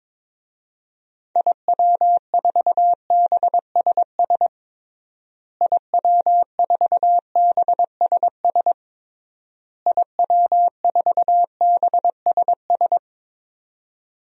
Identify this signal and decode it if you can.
Morse code, heard as a keyed tone, about 22 wpm, 700 Hz. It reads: IW4BSS IW4BSS IW4BSS
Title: DXCC-100 Call Signs - 6 Characters - Repeated 3x - 22wpm